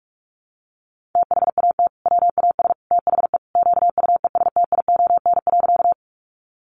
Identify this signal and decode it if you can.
Morse code, heard as a keyed tone, about 46 wpm, 700 Hz. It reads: THAT WAS THE QUESTION.
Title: Sentences from Top 400 Words 45wpm